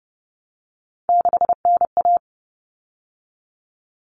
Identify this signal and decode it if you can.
Morse code, heard as a keyed tone, about 30 wpm, 700 Hz. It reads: 6NA
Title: Common Cut Numbers - 3-Digit 30wpm